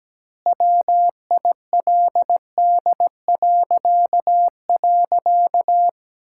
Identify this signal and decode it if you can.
Morse code, heard as a keyed tone, about 17 wpm, 700 Hz. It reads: WILD..
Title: Pangram Challenge v2 - 17wpm